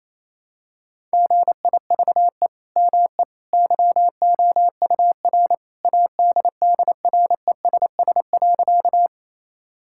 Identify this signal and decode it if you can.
Morse code, heard as a keyed tone, about 28 wpm, 700 Hz. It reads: GIVE ME YOUR ADDRESS.
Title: Sentences from Top 1000 Words - Review Part 6 - 28wpm